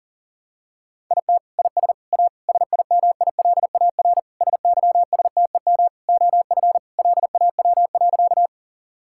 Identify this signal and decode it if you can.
Morse code, heard as a keyed tone, about 40 wpm, 700 Hz. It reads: IT IS A SIMILAR SYSTEM OF LAW.
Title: Sentences from Top 1000 Words - Review Part 5 - 40wpm